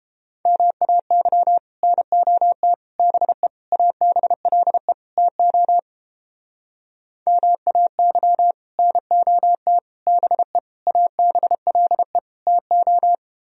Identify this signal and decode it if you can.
Morse code, heard as a keyed tone, about 31 wpm, 700 Hz. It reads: MAY NOT BE ABLE TO MAY NOT BE ABLE TO